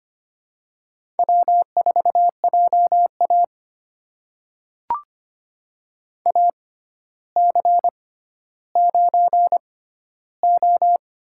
Morse code, heard as a keyed tone, about 25 wpm, spaced as for 13 wpm, 700 Hz. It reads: W4JA AC9O
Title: USA Call Signs - 4 Characters - 6x Character Spacing - 25wpm